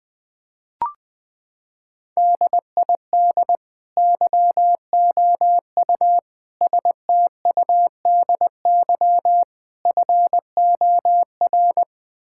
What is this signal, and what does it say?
Morse code, heard as a keyed tone, about 21 wpm, 700 Hz. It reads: DID YOU STUDY FOR
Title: Sentences from Top 1000 Words - Review Part 5 - 20wpm